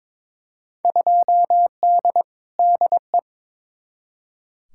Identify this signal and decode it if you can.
Morse code, heard as a keyed tone, about 22 wpm, 700 Hz. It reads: 2D DE
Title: Field Day Exchanges Encore - 22wpm